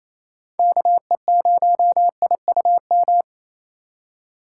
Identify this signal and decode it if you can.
Morse code, heard as a keyed tone, about 28 wpm, 700 Hz. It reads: KE0IUM
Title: USA Call Signs - 6 Characters - 3x Character Spacing - 28wpm